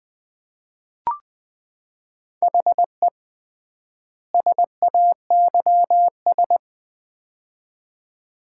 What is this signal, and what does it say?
Morse code, heard as a keyed tone, about 20 wpm, 700 Hz. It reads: HE SAYS
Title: 3x Word-Spacing - Sets of 2 Words - 20wpm